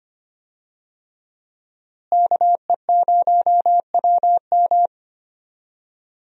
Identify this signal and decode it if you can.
Morse code, heard as a keyed tone, about 25 wpm, 700 Hz. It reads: KE0WM
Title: USA Call Signs - 5 Characters - Repeated 3x - 25wpm